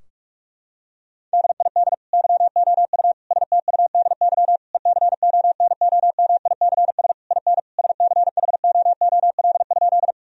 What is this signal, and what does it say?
Morse code, heard as a keyed tone, about 46 wpm, 700 Hz. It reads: DID YOU STUDY ECONOMICS IN SCHOOL?
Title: Sentences from Top 1000 Words - Review Part 3 - 45wpm